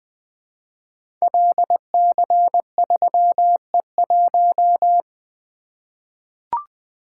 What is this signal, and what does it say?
Morse code, heard as a keyed tone, about 20 wpm, 700 Hz. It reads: LC3E1
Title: Code Groups - 5 Letter:Numbers - 20wpm